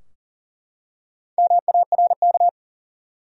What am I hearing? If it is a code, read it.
Morse code, heard as a keyed tone, about 40 wpm, 700 Hz. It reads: MARK
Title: USA Common Names - Max 4 Letters - 40wpm